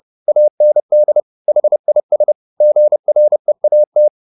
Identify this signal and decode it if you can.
Morse code, heard as a keyed tone, about 30 wpm, 600 Hz. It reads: AND HIS GREAT